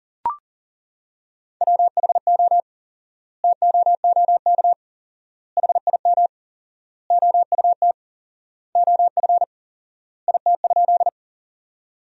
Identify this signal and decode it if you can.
Morse code, heard as a keyed tone, about 40 wpm, 700 Hz. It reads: WHO TOOK HIM OUT OF IT?
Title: 4x Word-Spacing - Sentences from Top 300 Words - 40wpm